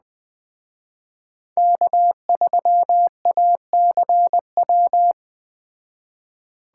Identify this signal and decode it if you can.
Morse code, heard as a keyed tone, about 20 wpm, 700 Hz. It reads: K3ACW
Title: Speed Racing - USA 1x3 Call Signs 17wpm